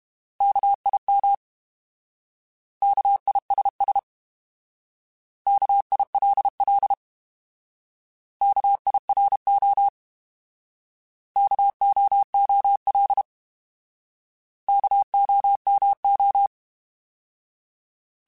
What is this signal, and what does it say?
Morse code, heard as a keyed tone, about 32 wpm, 800 Hz. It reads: KIM KISS KILL KIRO KOOL KOMO
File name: Alphabetical / Alphabetical mp3s / Ks @30